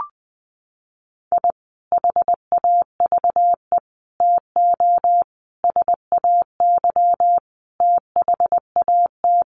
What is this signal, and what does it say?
Morse code, heard as a keyed tone, about 20 wpm, 700 Hz. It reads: I HAVE TO SAY THAT
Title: Sets of 5 Words Encore 20wpm